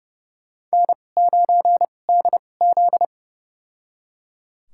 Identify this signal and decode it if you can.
Morse code, heard as a keyed tone, about 30 wpm, spaced as for 22 wpm, 700 Hz. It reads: N9DZ
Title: USA Call Signs - 4 Characters - 2x Character Spacing - 30wpm